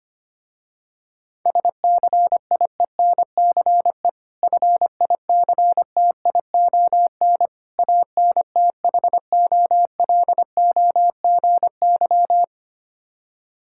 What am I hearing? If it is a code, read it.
Morse code, heard as a keyed tone, about 25 wpm, 700 Hz. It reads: SCIENCE FICTION ANTHOLOGY
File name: practice / 25wpm / Difficult Words with Hints - 25wpm